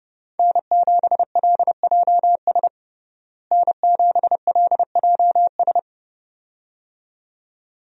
Morse code, heard as a keyed tone, about 30 wpm, 700 Hz. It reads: N7LJS N7LJS
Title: USA Call Signs - 5 Characters - Repeated 2x - 30wpm